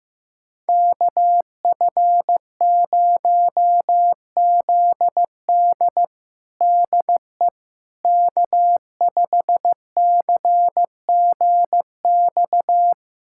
Morse code, heard as a keyed tone, about 15 wpm, 700 Hz. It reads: KF0ZD DE K5CGX
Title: USA Callsign DE Callsign - 5 Characters - 15wpm